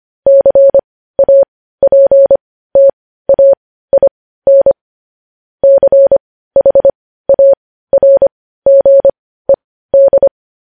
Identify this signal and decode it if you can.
Morse code, heard as a keyed tone, about 25 wpm, spaced as for 15 wpm, 550 Hz. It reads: CAPTAIN CHARGED
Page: news headlines in morse code at 15 wpm